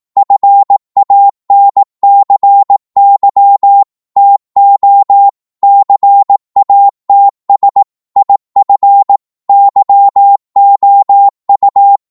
Morse code, heard as a keyed tone, about 18 wpm, 800 Hz. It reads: FANCY TO CATS IF YOU
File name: Morse / psg-18wpm